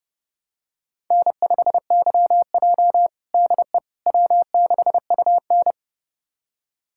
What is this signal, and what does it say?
Morse code, heard as a keyed tone, about 32 wpm, 700 Hz. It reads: N5YJ DE W6UN
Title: USA Callsign DE Callsign - 4 Characters - 30wpm